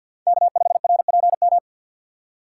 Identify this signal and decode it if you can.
Morse code, heard as a keyed tone, about 47 wpm, 700 Hz. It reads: K5RPM
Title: USA Call Signs - 3x Character Spacing - 50wpm